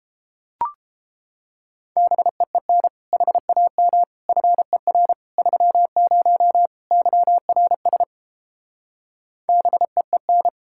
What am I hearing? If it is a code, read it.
Morse code, heard as a keyed tone, about 29 wpm, 700 Hz. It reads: BEEN HAM FER 30 YRS BEEN